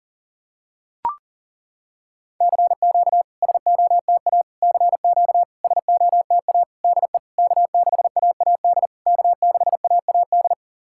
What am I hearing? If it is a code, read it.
Morse code, heard as a keyed tone, about 40 wpm, 700 Hz. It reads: CQ SOTA CQ SOTA DE K6AAD K6AAD
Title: SOTA QSO Practice 40wpm